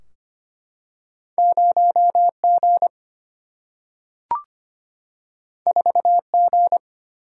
Morse code, heard as a keyed tone, about 25 wpm, 700 Hz. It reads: 0G 4G
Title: Single-Letter Single-Number 25wpm